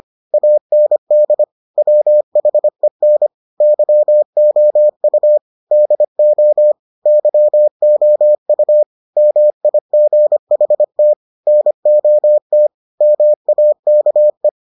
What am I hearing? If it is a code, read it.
Morse code, heard as a keyed tone, about 25 wpm, 600 Hz. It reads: AND WHEN YOU DO YOU MIGHT NOT MAKE